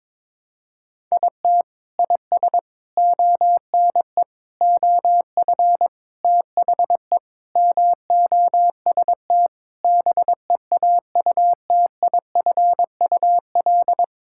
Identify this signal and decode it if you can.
Morse code, heard as a keyed tone, about 22 wpm, 700 Hz. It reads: IT IS ONE OF THE MOST BEAUTIFUL
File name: practice / 22wpm / Sentences with 3-Word Phrases - 22wpm